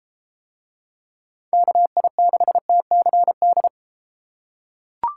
Morse code, heard as a keyed tone, about 31 wpm, 700 Hz. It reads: KI6TCD